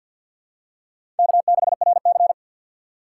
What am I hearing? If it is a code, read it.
Morse code, heard as a keyed tone, about 51 wpm, 700 Hz. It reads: K6RC
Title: K1USN SST QSO Practice - 50wpm